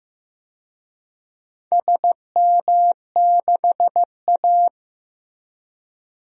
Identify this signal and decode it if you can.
Morse code, heard as a keyed tone, about 15 wpm, 700 Hz. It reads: SM6A
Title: DXCC-100 Call Signs - 4 Characters - Repeated 2x - 15wpm